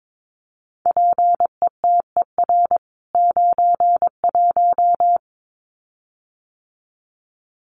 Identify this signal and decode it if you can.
Morse code, heard as a keyed tone, about 22 wpm, 700 Hz. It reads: PETER 91
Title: CWT Exchange 22wpm